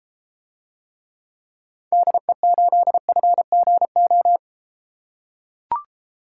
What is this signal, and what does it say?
Morse code, heard as a keyed tone, about 32 wpm, 700 Hz. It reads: DE8FGO